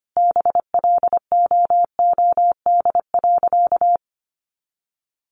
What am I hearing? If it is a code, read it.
Morse code, heard as a keyed tone, about 25 wpm, 700 Hz. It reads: BLOOD.
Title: Sentences from Top 1100 Words 25wpm